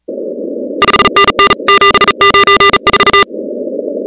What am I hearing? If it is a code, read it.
Morse code, heard as a keyed tone, about 36 wpm, 1200 Hz, with noise. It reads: 5NN 794
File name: contests / 2019-cw-wpx-kt5j-k5pi- -2330 0-16